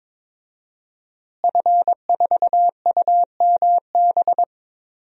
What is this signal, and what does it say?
Morse code, heard as a keyed tone, about 21 wpm, 700 Hz. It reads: F4UMB
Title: DXCC-100 Call Signs - 5 Characters - Repeated 3x - 22wpm